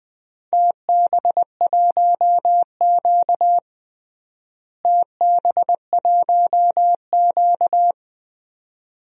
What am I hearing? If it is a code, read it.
Morse code, heard as a keyed tone, about 19 wpm, 700 Hz. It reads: TB1Q TB1Q
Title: DXCC-100 Call Signs - 4 Characters - Repeated 2x - 20wpm